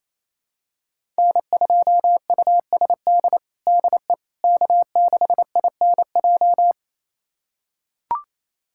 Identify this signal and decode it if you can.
Morse code, heard as a keyed tone, about 28 wpm, 700 Hz. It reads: N2USD DE K6INJ
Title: USA Callsign DE Callsign - 5 Characters - 28wpm